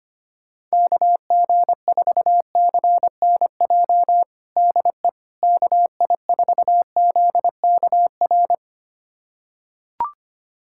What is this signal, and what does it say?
Morse code, heard as a keyed tone, about 25 wpm, 700 Hz. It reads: KG4CNJ DE KI4ZKR
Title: USA Callsign DE Callsign - 6 Characters - 25wpm